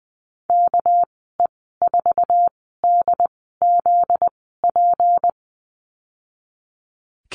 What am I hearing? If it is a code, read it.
Morse code, heard as a keyed tone, about 20 wpm, spaced as for 14 wpm, 700 Hz. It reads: KE4DZP
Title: 245 Lesson 39g - Letter z - Call Signs 20wpm